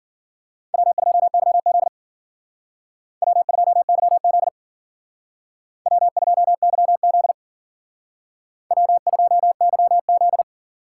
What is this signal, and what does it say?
Morse code, heard as a keyed tone, about 52 wpm, 700 Hz. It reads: W2YZ W2YZ W2YZ W2YZ